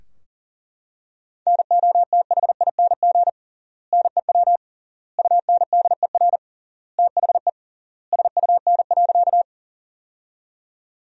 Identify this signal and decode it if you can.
Morse code, heard as a keyed tone, about 40 wpm, 700 Hz. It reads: NOTHING NEW UNDER THE SUN.
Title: 3x Word-Spacing - Sentences from Top 500 Words - 40wpm